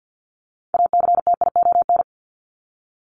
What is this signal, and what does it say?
Morse code, heard as a keyed tone, about 50 wpm, 700 Hz. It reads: ACTION
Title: Top 501-750 Words 50wpm